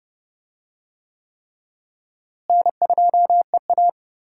Morse code, heard as a keyed tone, about 30 wpm, 700 Hz. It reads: N2EA
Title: USA Call Signs - 4 Characters - Repeated 3x - 30wpm